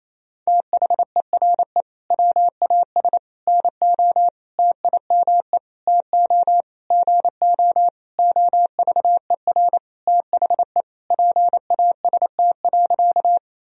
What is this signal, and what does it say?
Morse code, heard as a keyed tone, about 28 wpm, 700 Hz. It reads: THERE WAS NO TIME TO GO OVER THE PAST.
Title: Sentences from Top 1000 Words - Review Part 3 - 28wpm